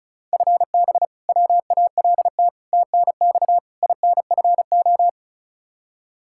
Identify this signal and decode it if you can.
Morse code, heard as a keyed tone, about 34 wpm, 700 Hz. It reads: FB WALT TNX INFO
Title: Practice QSO - Exchange 3 - More Info - 35wpm